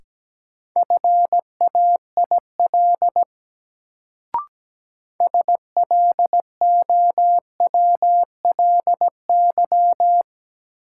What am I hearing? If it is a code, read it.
Morse code, heard as a keyed tone, about 17 wpm, 700 Hz. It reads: FAIL SLOWLY
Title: Sentences from Top 1100 Words - New Word Drill - 17wpm